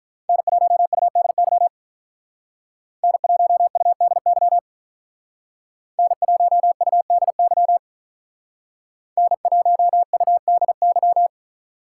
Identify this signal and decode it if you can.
Morse code, heard as a keyed tone, about 44 wpm, 700 Hz. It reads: N1UDY N1UDY N1UDY N1UDY